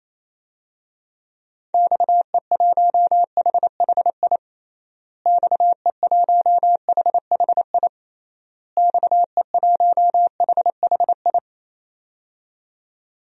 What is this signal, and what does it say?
Morse code, heard as a keyed tone, about 28 wpm, 700 Hz. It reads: XE1HHI XE1HHI XE1HHI
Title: DXCC-100 Call Signs - 6 Characters - Repeated 3x - 28wpm